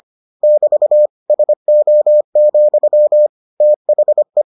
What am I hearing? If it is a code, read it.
Morse code, heard as a keyed tone, about 25 wpm, 600 Hz. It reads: <BT> SO, THE